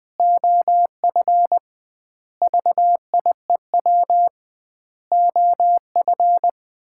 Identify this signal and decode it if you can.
Morse code, heard as a keyed tone, about 21 wpm, 700 Hz. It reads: OF VIEW OF
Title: 2x Word-Spacing - Sets of 5 Words - 20wpm